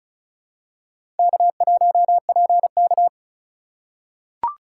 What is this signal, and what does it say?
Morse code, heard as a keyed tone, about 36 wpm, 700 Hz. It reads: K1PK
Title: USA Call Signs - 4 Characters - 3x Character Spacing - 35wpm